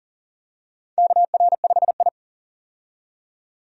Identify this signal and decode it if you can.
Morse code, heard as a keyed tone, about 37 wpm, 700 Hz. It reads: KR5I
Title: USA Call Signs - 4 Characters - 40wpm